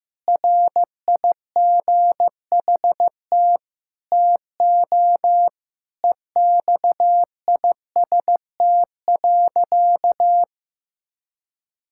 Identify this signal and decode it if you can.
Morse code, heard as a keyed tone, about 15 wpm, 700 Hz. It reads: RIGHT TO EXIST.
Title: Sentences from Top 1000 Words - Review Part 5 - 15wpm